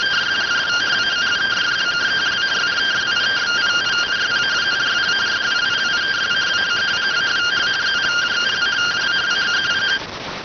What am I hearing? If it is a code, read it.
Two-tone teleprinter signal, about 170 Hz shift.